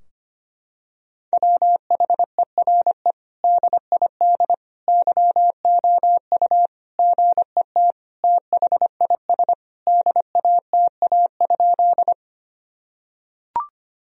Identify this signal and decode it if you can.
Morse code, heard as a keyed tone, about 25 wpm, 700 Hz. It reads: WHERE DID YOU GET THIS DATA?
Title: Sentences from Top 1000 Words - Review Part 3 - 25wpm